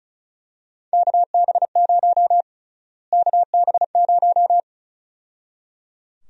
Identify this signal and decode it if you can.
Morse code, heard as a keyed tone, about 36 wpm, 700 Hz. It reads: KB0 KB0
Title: USA Call Sign Prefixes - Repeated 2x - 35wpm